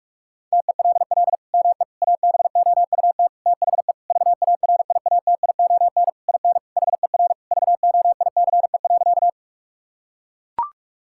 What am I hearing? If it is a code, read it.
Morse code, heard as a keyed tone, about 41 wpm, 700 Hz. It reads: TELL ME ABOUT THE VARIATION IN HER VOICE.
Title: Sentences from Top 1600 Words 45wpm